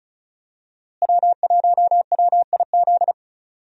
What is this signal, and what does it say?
Morse code, heard as a keyed tone, about 36 wpm, 700 Hz. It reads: W1WIZ
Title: USA Call Signs - 5 Characters - Repeated 3x - 35wpm